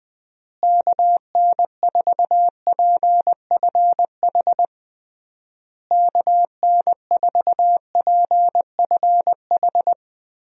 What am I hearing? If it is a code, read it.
Morse code, heard as a keyed tone, about 20 wpm, 700 Hz. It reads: KN4PFH KN4PFH
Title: USA Call Signs - 6 Characters - Repeated 3x - 20wpm